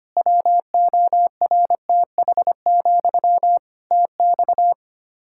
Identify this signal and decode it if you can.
Morse code, heard as a keyed tone, about 25 wpm, 700 Hz. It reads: WORTH, TX
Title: Practice QSO - Exchange 1 - Beginning - 25wpm